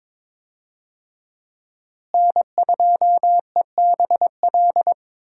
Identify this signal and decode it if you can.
Morse code, heard as a keyed tone, about 22 wpm, 700 Hz. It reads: N2EBL
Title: USA Call Signs - 5 Characters - Repeated 2x - 22wpm